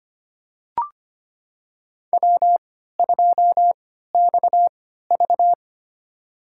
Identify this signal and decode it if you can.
Morse code, heard as a keyed tone, about 25 wpm, spaced as for 14 wpm, 700 Hz. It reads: W2XV
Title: USA Call Signs - 3x Character Spacing - 25wpm